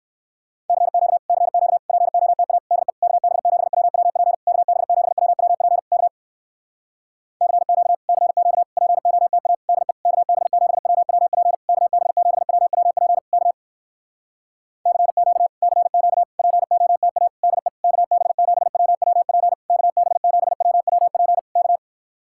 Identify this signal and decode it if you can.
Morse code, heard as a keyed tone, about 59 wpm, 700 Hz. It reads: CQ CQ POTA DE KB7WWP KB7WWP K CQ CQ POTA DE KB7WWP KB7WWP K CQ CQ POTA DE KB7WWP KB7WWP K